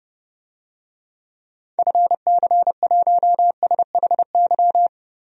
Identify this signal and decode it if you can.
Morse code, heard as a keyed tone, about 30 wpm, 700 Hz. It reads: FC1SHY
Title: DXCC-100 Call Signs - 6 Characters - 30wpm